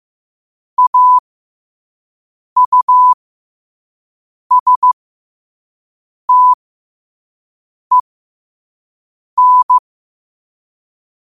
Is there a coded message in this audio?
Yes: Morse code from 0.8 to 9.8 s, heard as a keyed tone, about 15 wpm, spaced as for 5 wpm, 1000 Hz. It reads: AUSTEN